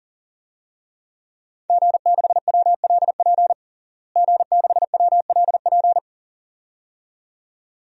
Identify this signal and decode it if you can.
Morse code, heard as a keyed tone, about 37 wpm, 700 Hz. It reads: G6WLP G6WLP
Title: DXCC-100 Call Signs - 5 Characters - Repeated 2x - 40wpm